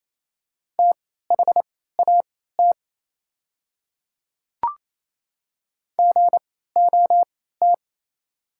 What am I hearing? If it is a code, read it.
Morse code, heard as a keyed tone, about 28 wpm, spaced as for 16 wpm, 700 Hz. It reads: THAT GOT
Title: 3x Character Spacing - Top 300 Words - 28wpm